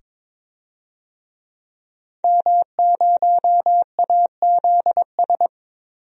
Morse code, heard as a keyed tone, about 21 wpm, 700 Hz. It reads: M0AZS
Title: DXCC-100 Call Signs - 5 Characters - 22wpm